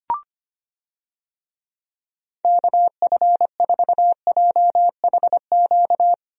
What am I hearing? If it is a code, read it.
Morse code, heard as a keyed tone, about 25 wpm, 700 Hz. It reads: KF4JHQ
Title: USA Call Signs - 6 Characters - Repeated 3x - 25wpm